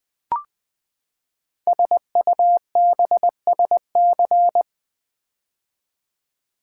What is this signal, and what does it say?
Morse code, heard as a keyed tone, about 20 wpm, 700 Hz. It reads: SUBSC
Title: Build A Word - 5 Letters At A Time - 12-Letter Words - 20wpm